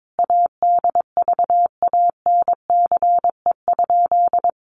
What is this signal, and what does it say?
Morse code, heard as a keyed tone, about 22 wpm, 700 Hz. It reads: ADVANCE?
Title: Sentences from Top 1400 Words 22wpm